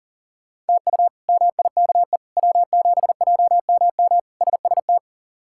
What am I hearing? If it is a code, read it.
Morse code, heard as a keyed tone, about 40 wpm, 700 Hz. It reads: TU MIKE W7JMM SST
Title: K1USN SST QSO Practice - 40wpm